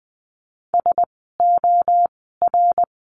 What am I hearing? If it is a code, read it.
Morse code, heard as a keyed tone, about 20 wpm, spaced as for 15 wpm, 700 Hz. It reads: SOR
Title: 061 Lesson 11f - Letter h - Words 20wpm